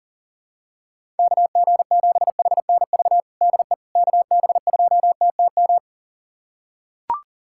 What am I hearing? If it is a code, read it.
Morse code, heard as a keyed tone, about 40 wpm, 700 Hz. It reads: KG7HNV DE KB2TTM
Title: USA Callsign DE Callsign - 6 Characters - 40wpm